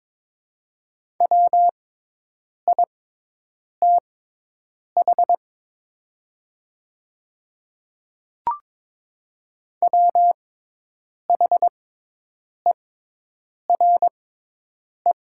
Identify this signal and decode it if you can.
Morse code, heard as a keyed tone, about 22 wpm, spaced as for 8 wpm, 700 Hz. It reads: WITH WHERE
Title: 6x Character Spacing - Top 300 Words - 22wpm